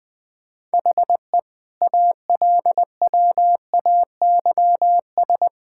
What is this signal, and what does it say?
Morse code, heard as a keyed tone, about 20 wpm, 700 Hz. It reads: HE ALWAYS
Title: Sentences from Top 1600 Words 20wpm